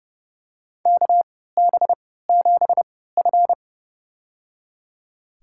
Morse code, heard as a keyed tone, about 31 wpm, spaced as for 17 wpm, 700 Hz. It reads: KB7F
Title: USA Call Signs - 3x Character Spacing - 30wpm